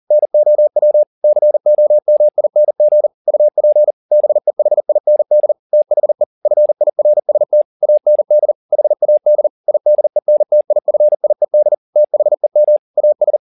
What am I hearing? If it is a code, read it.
Morse code, heard as a keyed tone, about 40 wpm, 600 Hz. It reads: NOW COMING UP BEHIND THE FIRST AND HAD IDENTIFIED THEM AS
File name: the old man and the sea 2 40wpm